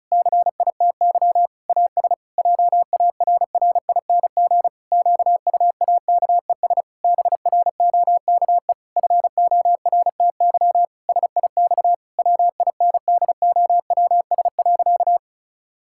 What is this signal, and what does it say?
Morse code, heard as a keyed tone, about 36 wpm, 700 Hz. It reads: CITY AS JARRING QUAKES BROKE FORTY SIX WINDOWS.
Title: Pangram Challenge v2 - 35wpm